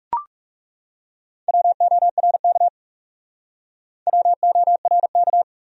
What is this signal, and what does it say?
Morse code, heard as a keyed tone, about 38 wpm, 700 Hz. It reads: WORK WORK